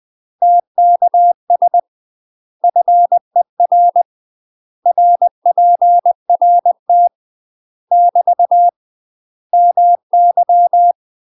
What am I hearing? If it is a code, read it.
Morse code, heard as a keyed tone, about 20 wpm, 700 Hz. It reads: TKS FER RPRT <BT> MY
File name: UK MASTER QSOs TRAINING / 20WPM / UK MASTER QSOs 0041